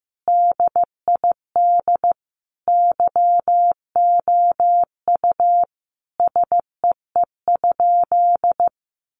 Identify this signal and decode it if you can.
Morse code, heard as a keyed tone, about 15 wpm, 700 Hz. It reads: DID YOU SEE?
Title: Sentences from Top 100 Words 15wpm